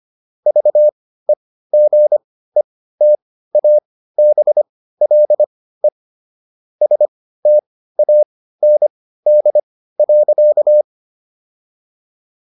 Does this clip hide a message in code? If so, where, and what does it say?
Yes: Morse code from 0.5 to 10.8 s, heard as a keyed tone, about 25 wpm, spaced as for 15 wpm, 600 Hz. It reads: VEGETABLE STAND.